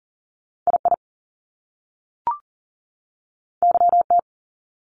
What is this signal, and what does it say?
Morse code, heard as a keyed tone, about 39 wpm, 700 Hz. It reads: I I Y T
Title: Top Two Letter Combinations in Words 40wpm